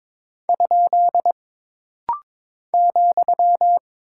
Morse code, heard as a keyed tone, about 22 wpm, 700 Hz. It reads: ? ,
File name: practice / 22wpm / Single Punctuation-Prosign - Rapid-Fire - 22wpm